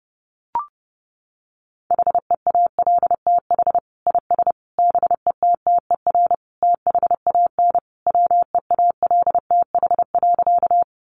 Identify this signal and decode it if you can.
Morse code, heard as a keyed tone, about 30 wpm, 700 Hz. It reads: HEALTH IS BETTER THAN WEALTH.
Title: Sentences from Top 1200 Words 30wpm